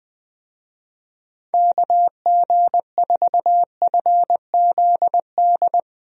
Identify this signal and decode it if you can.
Morse code, heard as a keyed tone, about 19 wpm, 700 Hz. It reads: KG4FZD
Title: USA Call Signs - 6 Characters - Repeated 2x - 20wpm